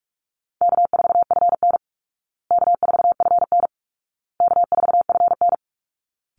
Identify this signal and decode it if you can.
Morse code, heard as a keyed tone, about 44 wpm, 700 Hz. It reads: K4FN K4FN K4FN
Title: Easy Call Signs Repeated 3x - 45wpm